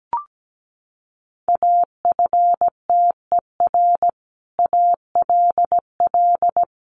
Morse code, heard as a keyed tone, about 17 wpm, 700 Hz. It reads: AFTER ALL
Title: Sentences from Top 1200 Words 17wpm